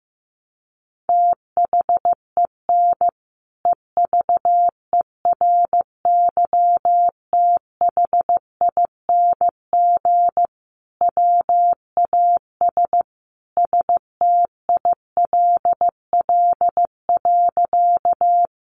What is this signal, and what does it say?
Morse code, heard as a keyed tone, about 15 wpm, 700 Hz. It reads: THEN EVERYTHING WAS STILL.
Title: Sentences from Top 700 Words 15wpm